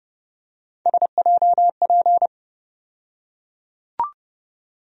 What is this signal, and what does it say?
Morse code, heard as a keyed tone, about 32 wpm, 700 Hz. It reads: SJP
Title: USA Call Sign Postfixes - 3 Letters 30wpm